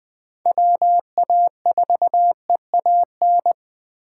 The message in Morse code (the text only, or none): WA4EAN